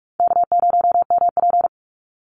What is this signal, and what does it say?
Morse code, heard as a keyed tone, about 44 wpm, 700 Hz. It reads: K0MP
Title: Easy Call Signs Repeated 3x - 45wpm